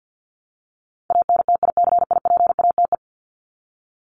Morse code, heard as a keyed tone, about 50 wpm, 700 Hz. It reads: ANTICIPATE
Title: Top 1251-1500 Words 50wpm